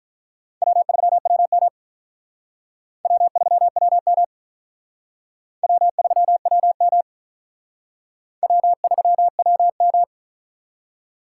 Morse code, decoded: W3WM W3WM W3WM W3WM

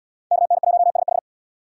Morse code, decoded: KN1SH

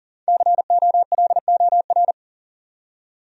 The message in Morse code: COLOR

